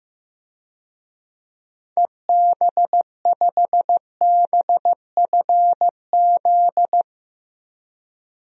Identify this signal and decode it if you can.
Morse code, heard as a keyed tone, about 16 wpm, 700 Hz. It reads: EB5BFZ
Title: DXCC-100 Call Signs - 6 Characters - Repeated 2x - 15wpm